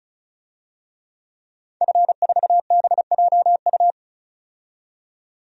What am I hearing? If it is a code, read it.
Morse code, heard as a keyed tone, about 34 wpm, 700 Hz. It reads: F4BJU
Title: DXCC-100 Call Signs - 5 Characters - 35wpm